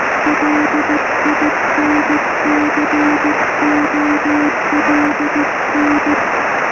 Morse code, heard as a keyed tone, about 15 wpm, 310 Hz, with noise. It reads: LINCOLN
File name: sm7lcb / g4cch 23cm